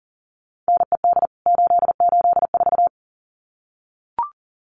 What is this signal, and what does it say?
Morse code, heard as a keyed tone, about 40 wpm, 700 Hz. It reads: NED 884
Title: CWT Exchange 40wpm